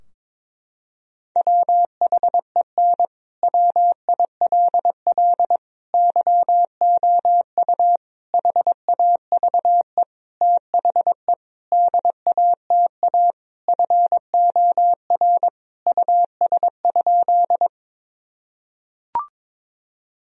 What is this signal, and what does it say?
Morse code, heard as a keyed tone, about 23 wpm, 700 Hz. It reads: WHEN WILL YOU HAVE THE DATA FOR US?
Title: Sentences from Top 1000 Words - Review Part 3 - 22wpm